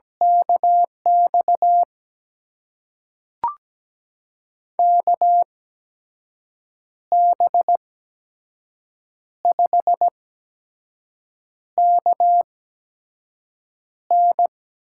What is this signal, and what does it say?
Morse code, heard as a keyed tone, about 17 wpm, spaced as for 5 wpm, 700 Hz. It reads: KX KB5KN
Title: USA Call Signs - 6 Characters - 8x Character Spacing - 17wpm